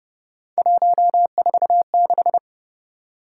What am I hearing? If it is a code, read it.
Morse code, heard as a keyed tone, about 29 wpm, 700 Hz. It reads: 146
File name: Speed-Racing - Three Letter-Number Combinations - 25wpm